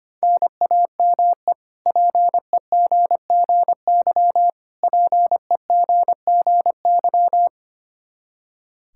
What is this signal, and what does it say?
Morse code, heard as a keyed tone, about 25 wpm, 700 Hz. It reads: NAME PEGGY PEGGY
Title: K1USN SST QSO Practice 25wpm